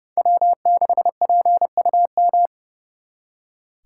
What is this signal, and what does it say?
Morse code, heard as a keyed tone, about 35 wpm, 700 Hz. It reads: W6PUM
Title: Speed Racing - USA 1x3 Call Signs 30wpm